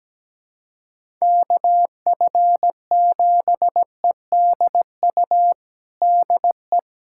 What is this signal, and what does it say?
Morse code, heard as a keyed tone, about 17 wpm, 700 Hz. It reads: KF7EDU DE
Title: USA Callsign DE Callsign - 6 Characters - 17wpm